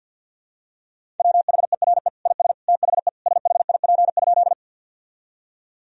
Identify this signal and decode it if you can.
Morse code, heard as a keyed tone, about 52 wpm, 700 Hz. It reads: WHERE IS THE SHIP?
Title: Sentences from Top 1000 Words - Review Part 2 - 50wpm